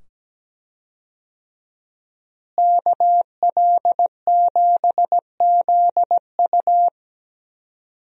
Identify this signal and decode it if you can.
Morse code, heard as a keyed tone, about 17 wpm, 700 Hz. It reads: KL7ZU
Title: USA Call Signs - 5 Characters - Repeated 2x - 17wpm